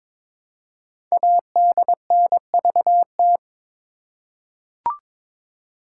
Morse code, heard as a keyed tone, about 22 wpm, 700 Hz. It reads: ADNVT
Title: Code Groups - 5 Letters - 22wpm